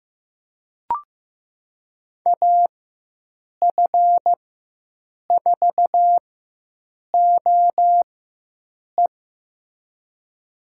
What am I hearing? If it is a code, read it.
Morse code, heard as a keyed tone, about 15 wpm, spaced as for 7 wpm, 700 Hz. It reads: AF4OE
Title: USA Call Signs - 4x Character Spacing - 15wpm